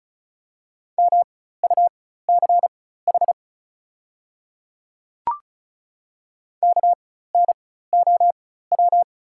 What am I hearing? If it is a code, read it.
Morse code, heard as a keyed tone, about 36 wpm, spaced as for 16 wpm, 690 Hz. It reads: MUCH KNOW
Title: 4x Character Spacing - Top 200 Words - 35wpm